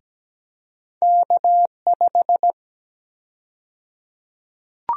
Morse code, heard as a keyed tone, about 17 wpm, 700 Hz. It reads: K5
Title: USA Call Sign Prefixes - Repeated 2x - 17wpm